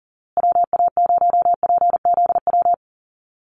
Morse code, heard as a keyed tone, about 40 wpm, 700 Hz. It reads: WA0PZW
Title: USA Call Signs Repeated 3x 40wpm